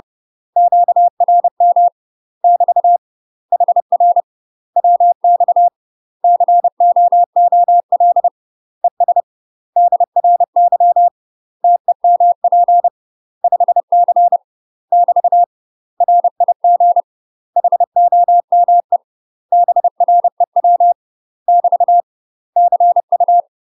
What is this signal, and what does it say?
Morse code, heard as a keyed tone, about 30 wpm, 700 Hz. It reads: QRM <BT> HR WX COOL ES DRY TEMP 5C <BT> RIG HOME BREW <BT> CU
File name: UK MASTER QSOs TRAINING / 30WPM / UK MASTER QSOs 0042